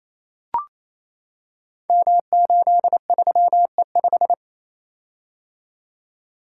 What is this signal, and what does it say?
Morse code, heard as a keyed tone, about 28 wpm, 700 Hz. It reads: M83E5
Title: Code Groups - 5 Letter:Numbers - 28wpm